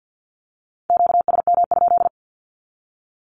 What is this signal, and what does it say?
Morse code, heard as a keyed tone, about 50 wpm, 700 Hz. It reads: QSM?
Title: Q-Codes with Spoken Meaning - 50wpm